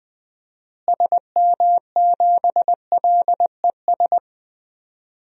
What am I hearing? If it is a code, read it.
Morse code, heard as a keyed tone, about 21 wpm, 700 Hz. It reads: SM7LES
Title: DXCC-100 Call Signs - 6 Characters - Repeated 2x - 20wpm